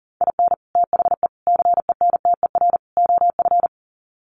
Morse code, heard as a keyed tone, about 40 wpm, 700 Hz. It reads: IN THE CENTER OF
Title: Sets of 4 Words 40wpm